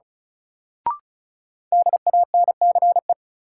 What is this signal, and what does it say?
Morse code, heard as a keyed tone, about 36 wpm, 700 Hz. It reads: DANCE
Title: Sentences from Top 700 Words - New Word Drill - 35wpm